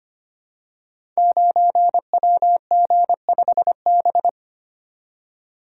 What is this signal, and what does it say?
Morse code, heard as a keyed tone, about 25 wpm, 700 Hz. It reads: 9WG5B
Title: Code Groups - 5 Letter:Numbers - 25wpm